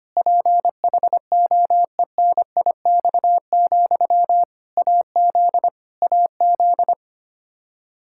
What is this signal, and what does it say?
Morse code, heard as a keyed tone, about 25 wpm, 700 Hz. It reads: PHOENIX, AZ AZ
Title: K1USN SST QSO Practice 25wpm